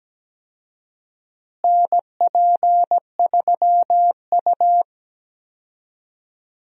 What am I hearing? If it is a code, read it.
Morse code, heard as a keyed tone, about 17 wpm, 700 Hz. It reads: NP3U
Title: USA Call Signs - 4 Characters - Repeated 2x - 17wpm